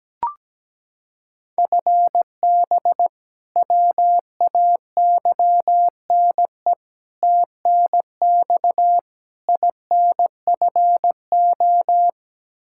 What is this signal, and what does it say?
Morse code, heard as a keyed tone, about 17 wpm, 700 Hz. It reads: FB WAYNE TNX INFO
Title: Practice QSO - Exchange 3 - More Info - 17wpm